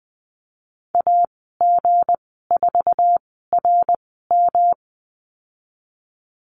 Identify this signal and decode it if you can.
Morse code, heard as a keyed tone, about 20 wpm, spaced as for 15 wpm, 700 Hz. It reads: AG4RM
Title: 167 Lesson 27g - Number 7 - Call Signs 20wpm